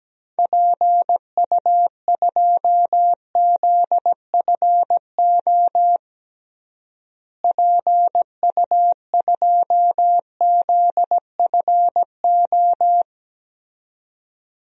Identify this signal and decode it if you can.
Morse code, heard as a keyed tone, about 17 wpm, 700 Hz. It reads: PU2ZFO PU2ZFO
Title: DXCC-100 Call Signs - 6 Characters - Repeated 2x - 17wpm